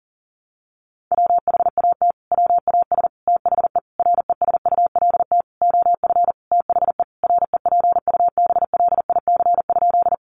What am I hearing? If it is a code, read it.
Morse code, heard as a keyed tone, about 40 wpm, 700 Hz. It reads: WHAT WAS THE RESULT OF THE REPUBLIC?
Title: Sentences from Top 1500 Words 40wpm